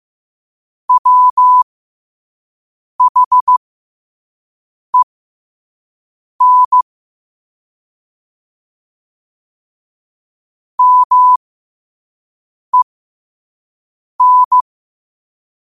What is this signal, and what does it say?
Morse code, heard as a keyed tone, about 15 wpm, spaced as for 5 wpm, 1000 Hz. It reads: WHEN MEN
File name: qotdfiles / 5wpm